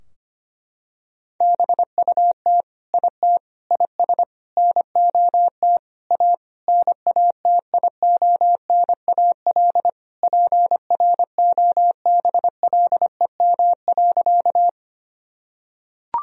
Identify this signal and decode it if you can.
Morse code, heard as a keyed tone, about 25 wpm, 700 Hz. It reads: BUT IT IS NOT A NATIONAL PROBLEM.
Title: Sentences from Top 1000 Words - Review Part 4 - 25wpm